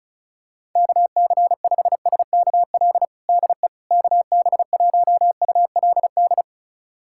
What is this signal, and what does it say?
Morse code, heard as a keyed tone, about 36 wpm, 700 Hz. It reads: KC5SKL DE KB1ULD
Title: USA Callsign DE Callsign - 6 Characters - 35wpm